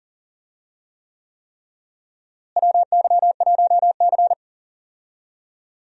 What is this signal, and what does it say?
Morse code, heard as a keyed tone, about 40 wpm, 700 Hz. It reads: WY1C
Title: USA Call Signs - 4 Characters - Repeated 3x - 40wpm